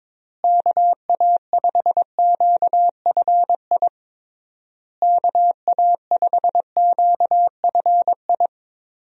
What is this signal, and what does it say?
Morse code, heard as a keyed tone, about 22 wpm, 700 Hz. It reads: KA5QFI KA5QFI
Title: USA Call Signs - 6 Characters - Repeated 3x - 22wpm